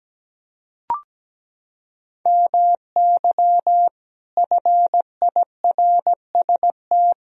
Morse code, heard as a keyed tone, about 17 wpm, 700 Hz. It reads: MY FIRST
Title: Sentences from Top 1000 Words - Review Part 3 - 17wpm